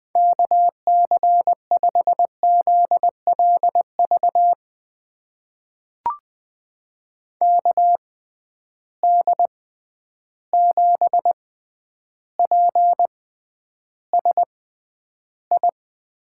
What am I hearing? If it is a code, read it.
Morse code, heard as a keyed tone, about 20 wpm, spaced as for 10 wpm, 700 Hz. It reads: KC5ZLV KD7PSI
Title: USA Call Signs - 6 Characters - 6x Character Spacing - 20wpm